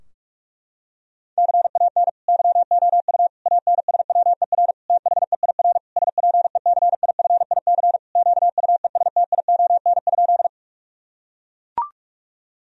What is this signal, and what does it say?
Morse code, heard as a keyed tone, about 42 wpm, 700 Hz. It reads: CAN YOU ANSWER THEIR SPECIFIC QUESTION?
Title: Sentences from Top 1000 Words - Review Part 3 - 45wpm